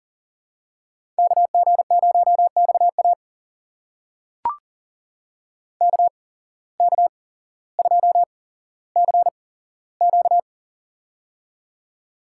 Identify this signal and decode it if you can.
Morse code, heard as a keyed tone, about 38 wpm, spaced as for 17 wpm, 700 Hz. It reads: KG0XA KK2CQ
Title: USA Call Signs - 5 Characters - 8x Character Spacing - 40wpm